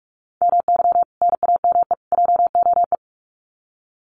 Morse code, heard as a keyed tone, about 44 wpm, 700 Hz. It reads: MY NAME JOE
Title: Common QSO Phrases 45wpm